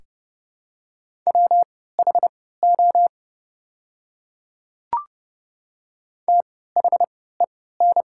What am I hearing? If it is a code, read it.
Morse code, heard as a keyed tone, about 30 wpm, spaced as for 17 wpm, 700 Hz. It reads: WHO THEN
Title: 3x Character Spacing - Top 100 Words - 30wpm